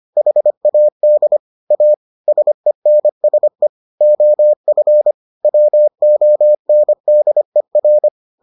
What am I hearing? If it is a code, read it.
Morse code, heard as a keyed tone, about 25 wpm, 600 Hz. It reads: HAD A SENSE OF WONDER